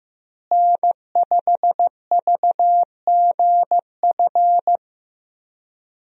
Morse code, heard as a keyed tone, about 15 wpm, 700 Hz. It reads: N5VGF
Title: USA Call Signs - 5 Characters - Repeated 3x - 15wpm